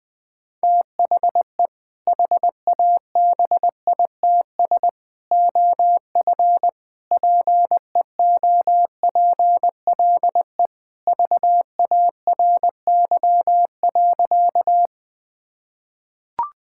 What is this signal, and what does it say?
Morse code, heard as a keyed tone, about 20 wpm, 700 Hz. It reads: THE HABITS OF PEOPLE VARY.
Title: Sentences with 2-Word Phrases - 20wpm